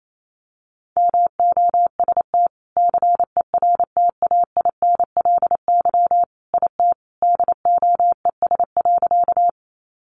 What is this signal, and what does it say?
Morse code, heard as a keyed tone, about 28 wpm, 700 Hz. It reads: MOST CERTAINLY IT DOES.
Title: Sentences from Top 900 Words 28wpm